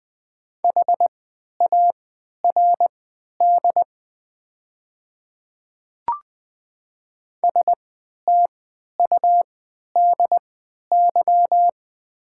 Morse code, heard as a keyed tone, about 20 wpm, spaced as for 11 wpm, 700 Hz. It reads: HARD STUDY
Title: 3x Character Spacing - Top 300 Words - 20wpm